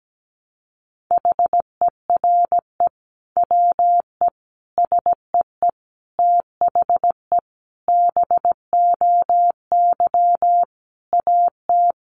Morse code, heard as a keyed tone, about 17 wpm, 700 Hz. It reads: HERE WE SEE THE BOY AT
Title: Sentences from Top 300 Words 17wpm